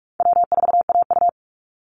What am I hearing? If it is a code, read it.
Morse code, heard as a keyed tone, about 46 wpm, 700 Hz. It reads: W4AU
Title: Easy Call Signs - 45wpm